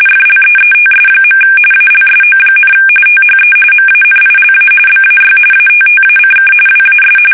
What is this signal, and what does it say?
RTTY, about 45.45 baud. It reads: FLEETS. ALSO, THE MANILA BAY ATTACK WAS
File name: fsk850 / usnavy3